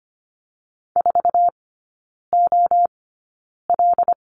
Morse code, heard as a keyed tone, about 25 wpm, spaced as for 9 wpm, 690 Hz. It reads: 4OL
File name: Lesson 14e - Number 2 - Three Characters 25wpm